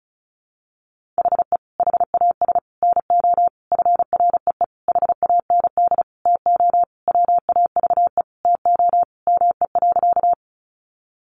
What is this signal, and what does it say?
Morse code, heard as a keyed tone, about 34 wpm, 700 Hz. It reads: HE HAS NO FREE HAND TO WAVE TO ME.
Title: Sentences from Top 1400 Words 35wpm